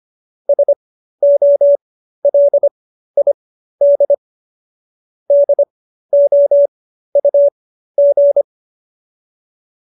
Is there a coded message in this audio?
Yes: Morse code from 0.5 to 8.4 s, heard as a keyed tone, about 25 wpm, spaced as for 13 wpm, 590 Hz. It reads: SOLID DOUG